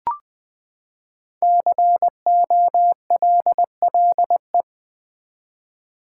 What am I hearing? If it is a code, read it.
Morse code, heard as a keyed tone, about 20 wpm, 700 Hz. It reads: COLLE
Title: Build A Word - 5 Letters At A Time - 10-Letter Words - 20wpm